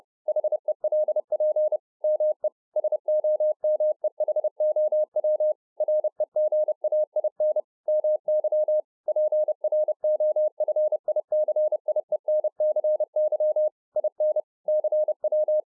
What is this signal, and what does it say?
Morse code, heard as a keyed tone, about 30 wpm, 610 Hz. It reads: HELP ME SOMEHOW REGAIN MY PROFICIENCY IN CW